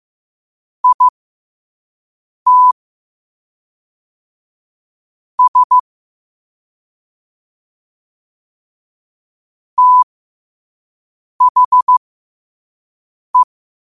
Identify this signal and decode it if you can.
Morse code, heard as a keyed tone, about 15 wpm, spaced as for 5 wpm, 1000 Hz. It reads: IT S THE
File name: qotdfiles / 5wpm